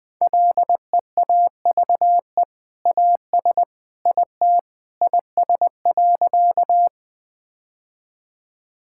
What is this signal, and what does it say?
Morse code, heard as a keyed tone, about 20 wpm, 700 Hz. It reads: LEAVE AS IT IS.
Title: Sentences with 2-Word Phrases - 20wpm